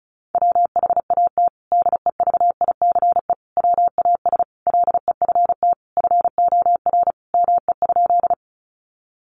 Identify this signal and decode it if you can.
Morse code, heard as a keyed tone, about 34 wpm, 700 Hz. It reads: WHAT DEVICE WAS LEFT FOR ME?
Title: Sentences from Top 1500 Words 35wpm